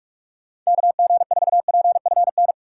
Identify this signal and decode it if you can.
Morse code, heard as a keyed tone, about 46 wpm, 700 Hz. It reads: KG4PFN
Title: USA Call Signs - 6 Characters - 4x Character Spacing - 45wpm